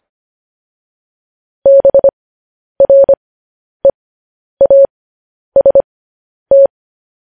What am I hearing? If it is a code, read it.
Morse code, heard as a keyed tone, about 25 wpm, spaced as for 10 wpm, 550 Hz. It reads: BREAST
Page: news headlines in morse code at 10 wpm